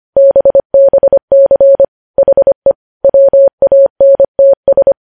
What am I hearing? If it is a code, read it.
Morse code, heard as a keyed tone, about 25 wpm, 550 Hz. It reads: BBC HE WANTS
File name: podcast / News headlines-1x-25x25